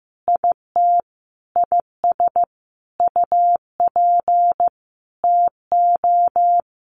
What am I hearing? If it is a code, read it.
Morse code, heard as a keyed tone, about 15 wpm, 700 Hz. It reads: IT IS UP TO
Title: Sentences from Top 1200 Words 15wpm